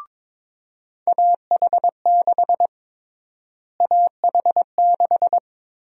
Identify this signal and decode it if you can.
Morse code, heard as a keyed tone, about 22 wpm, 700 Hz. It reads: AH6 AH6
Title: USA Call Sign Prefixes - Repeated 2x - 22wpm